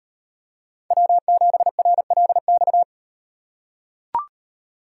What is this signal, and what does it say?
Morse code, heard as a keyed tone, about 38 wpm, 700 Hz. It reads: W7RLX